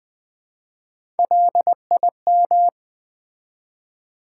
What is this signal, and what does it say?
Morse code, heard as a keyed tone, about 20 wpm, 700 Hz. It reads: LIM
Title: Build A Word - 1 Letter At A Time - 4-Letter Words - 20wpm